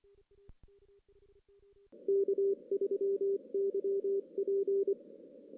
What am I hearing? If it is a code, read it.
Morse code, heard as a keyed tone, about 24 wpm, 410 Hz. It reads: K3YP
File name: 2016 JAN CW NAQP / 2016 Jan naqp k5ot - 0-5